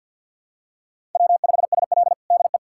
Morse code, heard as a keyed tone, about 46 wpm, 700 Hz. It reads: W5SL DE
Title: USA Callsign DE Callsign - 4 Characters - 50wpm